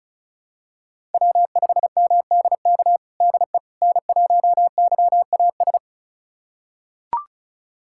Morse code, heard as a keyed tone, about 36 wpm, 700 Hz. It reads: W5MDK DE N1YAS
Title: USA Callsign DE Callsign - 5 Characters - 35wpm